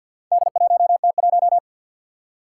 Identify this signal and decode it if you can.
Morse code, heard as a keyed tone, about 50 wpm, 700 Hz. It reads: D1T1
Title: Four Challenging Letter-Number Combinations 50wpm